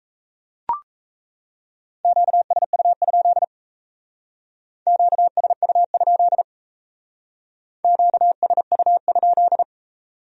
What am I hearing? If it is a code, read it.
Morse code, heard as a keyed tone, about 37 wpm, 700 Hz. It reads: QSU? QSU? QSU?